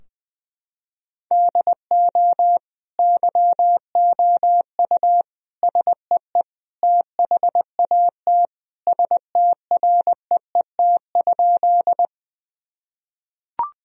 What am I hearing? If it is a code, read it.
Morse code, heard as a keyed tone, about 20 wpm, 700 Hz. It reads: DO YOU SEE THAT STREET?
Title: Sentences from Top 1000 Words - Review Part 3 - 20wpm